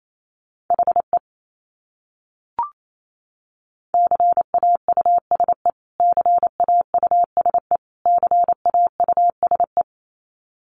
Morse code, heard as a keyed tone, about 28 wpm, 700 Hz. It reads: HE CAUSE CAUSE CAUSE
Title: Top 200 Words Repeated 28wpm